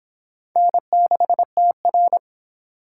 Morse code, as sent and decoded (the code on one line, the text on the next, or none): -. -.... - .-.
N6TR